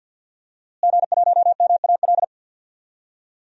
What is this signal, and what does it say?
Morse code, heard as a keyed tone, about 49 wpm, 700 Hz. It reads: G1MAL
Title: DXCC-100 Call Signs - 5 Characters - Repeated 3x - 50wpm